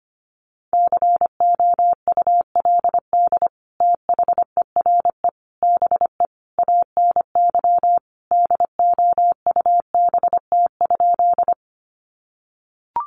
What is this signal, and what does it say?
Morse code, heard as a keyed tone, about 25 wpm, 700 Hz. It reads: COULD THERE BE ANY DOUBT?
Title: Sentences from Top 1500 Words 25wpm